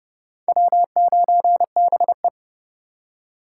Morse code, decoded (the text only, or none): W9BE